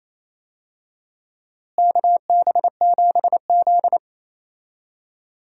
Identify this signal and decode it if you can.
Morse code, heard as a keyed tone, about 28 wpm, 700 Hz. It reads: KB7Z
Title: USA Call Signs - 4 Characters - Repeated 2x - 28wpm